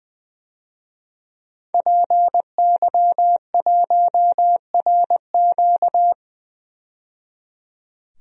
Morse code, heard as a keyed tone, about 20 wpm, 700 Hz. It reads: PY1RQ
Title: DXCC-100 Call Signs - 5 Characters - 20wpm